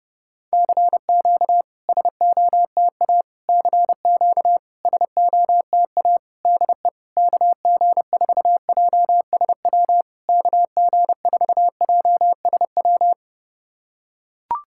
Morse code, heard as a keyed tone, about 30 wpm, 700 Hz. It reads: CQ SOTA CQ SOTA DE KG4JSW KG4JSW
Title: SOTA QSO Practice 30wpm